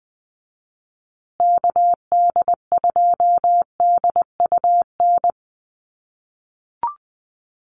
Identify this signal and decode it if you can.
Morse code, heard as a keyed tone, about 20 wpm, 700 Hz. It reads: KD2DUN
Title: USA Call Signs Repeated 2x 20wpm